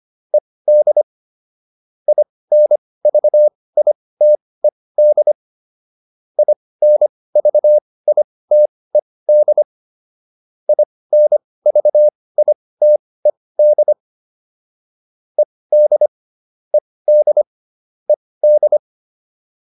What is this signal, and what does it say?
Morse code, heard as a keyed tone, about 25 wpm, spaced as for 18 wpm, 600 Hz. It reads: ED INVITED INVITED INVITED ED ED ED
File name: ED7 18